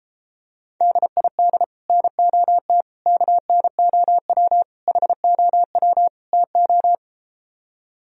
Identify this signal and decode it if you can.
Morse code, heard as a keyed tone, about 33 wpm, 700 Hz. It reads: DID NOT KNOW HOW TO